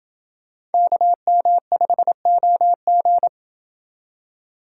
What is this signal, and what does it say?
Morse code, heard as a keyed tone, about 29 wpm, 700 Hz. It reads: KM5OG